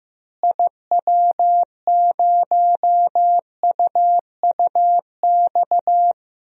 Morse code, heard as a keyed tone, about 15 wpm, 700 Hz. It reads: IW0UUX
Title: DXCC-100 Call Signs - 6 Characters - Repeated 2x - 15wpm